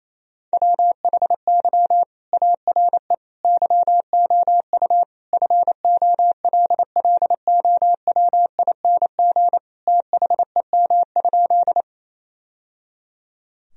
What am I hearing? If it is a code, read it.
Morse code, heard as a keyed tone, about 28 wpm, 700 Hz. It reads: WHY ARE YOU FOLLOWING THEM?
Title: Sentences from Top 1000 Words - Review Part 4 - 28wpm